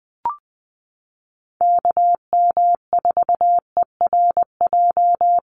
Morse code, heard as a keyed tone, about 20 wpm, 700 Hz. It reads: KM4ERJ
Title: USA Call Signs Repeated 2x 20wpm